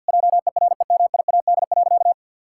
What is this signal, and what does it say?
Morse code, heard as a keyed tone, about 47 wpm, 700 Hz. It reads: JEREMIAD.
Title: Pangram Challenge v2 - 50wpm